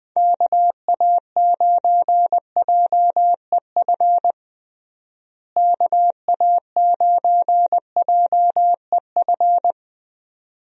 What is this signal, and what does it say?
Morse code, heard as a keyed tone, about 21 wpm, 700 Hz. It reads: KA9JEF KA9JEF
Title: USA Call Signs - 6 Characters - Repeated 2x - 20wpm